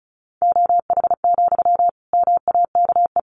Morse code, heard as a keyed tone, about 34 wpm, 700 Hz. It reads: OH, MAKE
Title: Sentences from Top 700 Words 35wpm